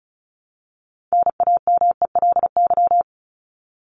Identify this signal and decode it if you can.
Morse code, heard as a keyed tone, about 34 wpm, 700 Hz. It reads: NAMELY
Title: LY Suffix - Top 100 Words with Base Words 35wpm